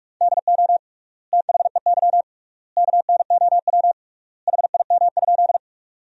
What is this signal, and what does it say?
Morse code, heard as a keyed tone, about 46 wpm, 690 Hz. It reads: DO THEY KNOW HIM?
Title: 3x Word-Spacing - Sentences from Top 100 Words - 45wpm